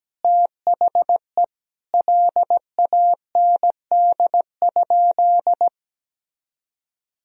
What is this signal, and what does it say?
Morse code, heard as a keyed tone, about 17 wpm, 700 Hz. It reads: THE LAND?
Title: Sentences from Top 1000 Words - Review Part 1 - 17wpm